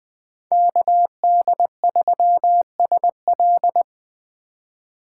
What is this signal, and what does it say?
Morse code, heard as a keyed tone, about 21 wpm, 700 Hz. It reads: KD3SL
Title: USA Call Signs - 5 Characters - 2x Character Spacing - 20wpm